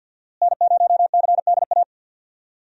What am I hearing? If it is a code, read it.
Morse code, heard as a keyed tone, about 48 wpm, 700 Hz. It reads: N0CDA
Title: USA Call Signs - 5 Characters - Repeated 3x - 50wpm